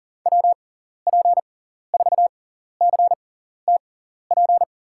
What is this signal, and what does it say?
Morse code, heard as a keyed tone, about 40 wpm, spaced as for 14 wpm, 700 Hz. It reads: WP4CTP
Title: USA Call Signs - 6 Characters - 6x Character Spacing - 40wpm